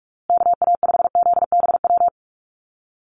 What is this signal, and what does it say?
Morse code, heard as a keyed tone, about 44 wpm, 700 Hz. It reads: KA5ZBW
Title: USA Call Signs Repeated 2x 45wpm